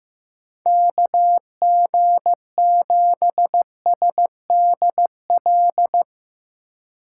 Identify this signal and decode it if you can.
Morse code, heard as a keyed tone, about 16 wpm, 700 Hz. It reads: KG7SDL
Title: Speed Racing - USA Call Signs 15wpm